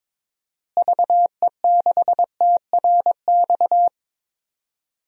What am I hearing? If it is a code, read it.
Morse code, heard as a keyed tone, about 22 wpm, 700 Hz. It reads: VE6TRX
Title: DXCC-100 Call Signs - 6 Characters - Repeated 2x - 22wpm